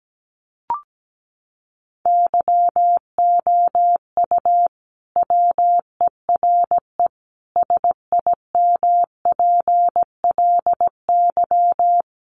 Morse code, heard as a keyed tone, about 17 wpm, 700 Hz. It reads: YOU WERE SIMPLY
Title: Sentences from Top 1100 Words 17wpm